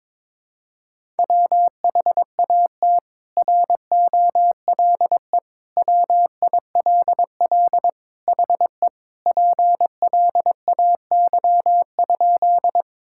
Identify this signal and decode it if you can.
Morse code, heard as a keyed tone, about 22 wpm, 700 Hz. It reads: WHAT ROLE WILL HE PLAY?
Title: Sentences from Top 1000 Words - Review Part 6 - 22wpm